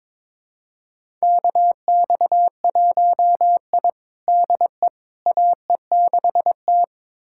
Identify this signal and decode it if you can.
Morse code, heard as a keyed tone, about 22 wpm, 700 Hz. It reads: KX1I DE AE6T
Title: USA Callsign DE Callsign - 4 Characters - 22wpm